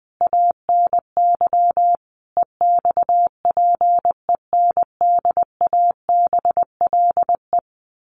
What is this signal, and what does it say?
Morse code, heard as a keyed tone, about 20 wpm, 700 Hz. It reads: ANY EXPENDABLE
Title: A Sound Of Thunder 20wpm